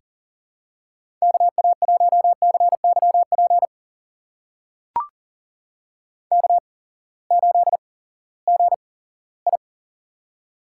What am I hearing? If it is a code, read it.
Morse code, heard as a keyed tone, about 38 wpm, spaced as for 20 wpm, 700 Hz. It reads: KA1CYP K8GI
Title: USA Call Signs - 8x Character Spacing - 40wpm